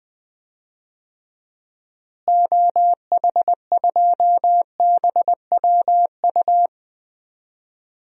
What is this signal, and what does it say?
Morse code, heard as a keyed tone, about 20 wpm, 700 Hz. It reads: OH2BWU
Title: DXCC-100 Call Signs - 6 Characters - Repeated 2x - 20wpm